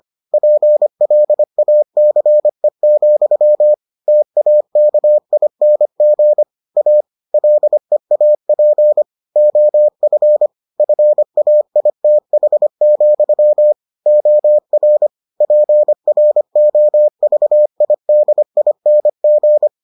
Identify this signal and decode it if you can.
Morse code, heard as a keyed tone, about 25 wpm, 600 Hz. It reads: PLACE, TAKING A LEAP OF FAITH, OR PROVIDING